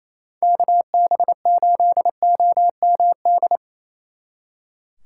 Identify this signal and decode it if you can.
Morse code, heard as a keyed tone, about 37 wpm, 700 Hz. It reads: KB8OMD